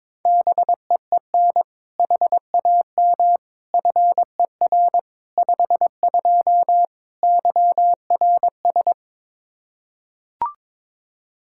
Morse code, heard as a keyed tone, about 22 wpm, 700 Hz. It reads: BEEN HAM FER 52 YRS
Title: Practice QSO - Exchange 3 - More Info - 22wpm